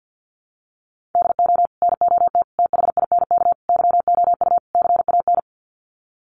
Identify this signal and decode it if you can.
Morse code, heard as a keyed tone, about 48 wpm, 700 Hz. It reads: DO NOT THINK YOU CAN
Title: Sets of 5 Words 50wpm